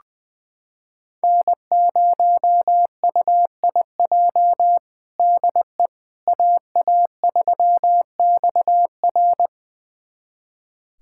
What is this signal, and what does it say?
Morse code, heard as a keyed tone, about 20 wpm, 700 Hz. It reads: N0UIJ DE AA3XR
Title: USA Callsign DE Callsign - 5 Characters - 20wpm